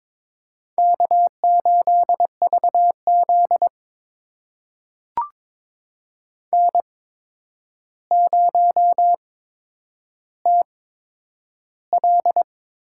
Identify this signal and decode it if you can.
Morse code, heard as a keyed tone, about 22 wpm, spaced as for 9 wpm, 700 Hz. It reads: K8VZ N0TL
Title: USA Call Signs - 4 Characters - 8x Character Spacing - 22wpm